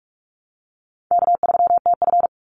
Morse code, heard as a keyed tone, about 44 wpm, 700 Hz. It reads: K3TF
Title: Easy Call Signs Repeated 2x - 45wpm